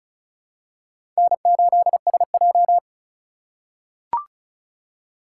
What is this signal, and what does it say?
Morse code, heard as a keyed tone, about 34 wpm, 700 Hz. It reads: N8SJ
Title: USA Call Signs - 4 Characters - 3x Character Spacing - 35wpm